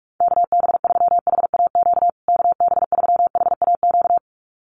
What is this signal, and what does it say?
Morse code, heard as a keyed tone, about 44 wpm, 690 Hz. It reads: KB3HAQ KB3HAQ
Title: USA Call Signs Repeated 2x 45wpm